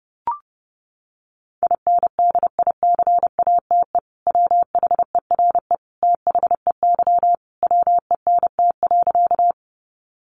Morse code, heard as a keyed tone, about 30 wpm, 700 Hz. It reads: INDICATE WHERE THEY WENT.
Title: Sentences from Top 1300 Words 30wpm